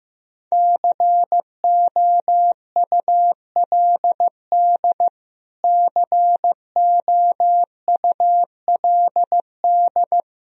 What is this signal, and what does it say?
Morse code, heard as a keyed tone, about 15 wpm, 700 Hz. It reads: COULD COULD
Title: Top 100 Words Repeated 15wpm